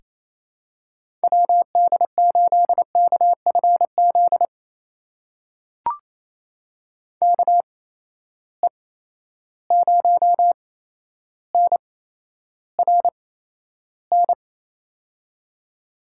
Morse code, heard as a keyed tone, about 28 wpm, spaced as for 12 wpm, 700 Hz. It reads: WD8KFZ KE0NRN
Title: USA Call Signs - 6 Characters - 8x Character Spacing - 28wpm